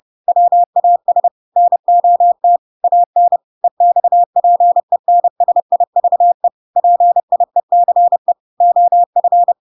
Morse code, heard as a keyed tone, about 30 wpm, 700 Hz. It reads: WAS NOT AN EXPENSIVE PIECE OF
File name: Pinocchio / Pinocchio ch1 30 30 ik1hge